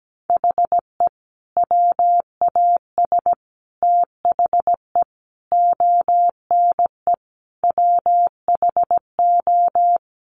Morse code, heard as a keyed tone, about 17 wpm, 700 Hz. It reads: HE WAS THE ONE WHO
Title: Sets of 5 Words Encore2 17wpm